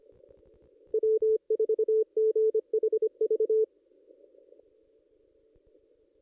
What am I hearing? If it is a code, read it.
Morse code, heard as a keyed tone, about 25 wpm, 440 Hz. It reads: W4GHV
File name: contests / ss-cw-k5tr-op-n5rz -1900